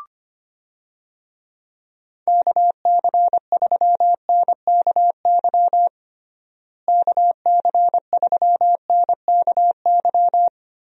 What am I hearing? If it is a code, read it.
Morse code, heard as a keyed tone, about 25 wpm, 700 Hz. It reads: KC3NKY KC3NKY
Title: USA Call Signs - 6 Characters - Repeated 3x - 25wpm